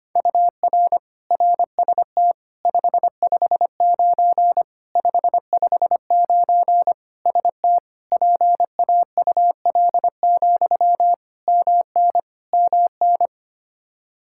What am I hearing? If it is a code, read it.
Morse code, heard as a keyed tone, about 25 wpm, 700 Hz. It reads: UR RST 559 559 ST PAUL, MN MN
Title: Practice QSO - Exchange 1 - Beginning - 25wpm